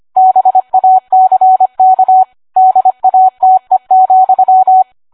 Morse code, heard as a keyed tone, about 25 wpm, 750 Hz, with radio noise. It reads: BACK DATE,